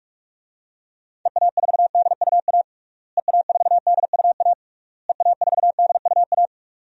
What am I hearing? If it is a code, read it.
Morse code, heard as a keyed tone, about 46 wpm, 700 Hz. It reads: EA4DUA EA4DUA EA4DUA
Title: DXCC-100 Call Signs - 6 Characters - Repeated 3x - 45wpm